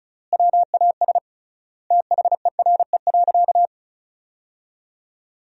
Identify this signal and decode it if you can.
Morse code, heard as a keyed tone, about 36 wpm, 700 Hz. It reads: WAS THERE.
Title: 3x Word-Spacing - Sentences from Top 500 Words - 35wpm